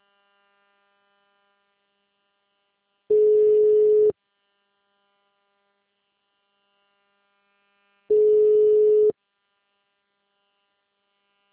Ringback tone (Europe, 425 Hz).